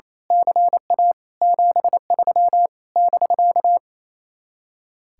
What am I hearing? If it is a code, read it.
Morse code, heard as a keyed tone, about 27 wpm, 700 Hz. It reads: CA 73 <BK>
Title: Speed-Racing Common QSO Phrases - 28wpm